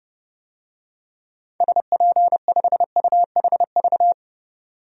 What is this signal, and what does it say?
Morse code, heard as a keyed tone, about 32 wpm, 700 Hz. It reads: SP5UHV
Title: DXCC-100 Call Signs - 6 Characters - 30wpm